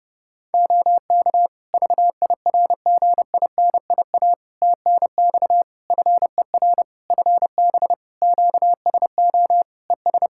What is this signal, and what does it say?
Morse code, heard as a keyed tone, about 30 wpm, 700 Hz. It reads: OK VIRGINIA TNX FER FB QSO ES
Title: Practice QSO - Exchange 4 - Ending - 30wpm